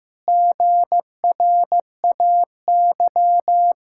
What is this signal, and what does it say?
Morse code, heard as a keyed tone, about 16 wpm, 700 Hz. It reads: GRAY
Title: Pangram Challenge v2 - 15wpm